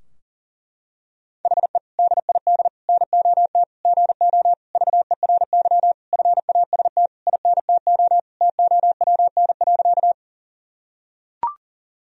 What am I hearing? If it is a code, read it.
Morse code, heard as a keyed tone, about 40 wpm, 700 Hz. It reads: HE DID NOT GO VERY FAST INTO TOWN.
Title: Sentences from Top 1000 Words - Review Part 2 - 40wpm